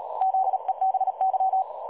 Morse code, heard as a keyed tone, about 38 wpm, 760 Hz. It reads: ZEBY
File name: dlazaawansowanych / nagraniazpasma / websdr recording 3565.0kHz